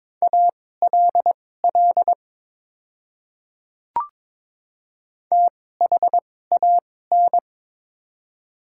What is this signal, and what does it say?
Morse code, heard as a keyed tone, about 22 wpm, spaced as for 16 wpm, 700 Hz. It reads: ALL THAN
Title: 2x Character Spacing - Top 100 Words - 22wpm